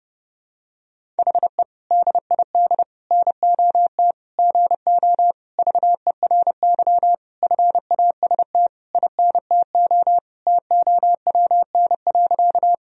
Morse code, heard as a keyed tone, about 30 wpm, 700 Hz. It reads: HE DID NOT GO VERY FAST INTO TOWN.
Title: Sentences from Top 1000 Words - Review Part 2 - 30wpm